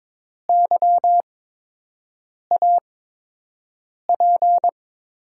Morse code, heard as a keyed tone, about 22 wpm, spaced as for 6 wpm, 700 Hz. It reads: YAP
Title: USA Call Signs - 5 Characters - 8x Character Spacing - 22wpm